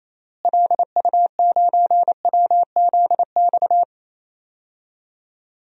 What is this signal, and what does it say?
Morse code, heard as a keyed tone, about 27 wpm, 700 Hz. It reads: LU9WZX
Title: DXCC-100 Call Signs - 6 Characters - Repeated 3x - 28wpm